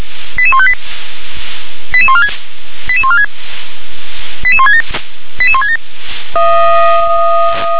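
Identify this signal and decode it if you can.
Five-tone selective call (ZVEI), heard as quick runs of five tones, tones of about 70 ms. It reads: 88158 88146 88146 88167 88167
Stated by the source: Received on an SDR as a ZVEI five-tone call.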